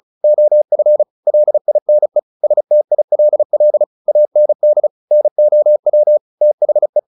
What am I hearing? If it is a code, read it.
Morse code, heard as a keyed tone, about 34 wpm, 600 Hz. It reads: OF LINE STILL AND NOW THE
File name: the old man and the sea 2 35wpm